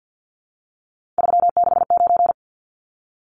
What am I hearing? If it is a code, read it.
Morse code, heard as a keyed tone, about 50 wpm, 700 Hz. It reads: 369
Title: Numbers - 3-Digit 50wpm